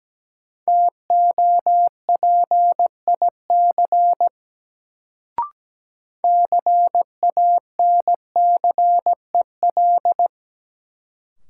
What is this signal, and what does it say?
Morse code, heard as a keyed tone, about 17 wpm, 700 Hz. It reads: TOPIC CANCEL
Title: Sentences from Top 1200 Words - New Word Drill - 17wpm